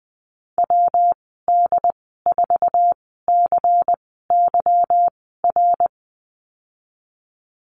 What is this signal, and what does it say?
Morse code, heard as a keyed tone, about 20 wpm, spaced as for 15 wpm, 700 Hz. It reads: WD4CYR
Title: 173 Lesson 28g - Number 9 - Call Signs 20wpm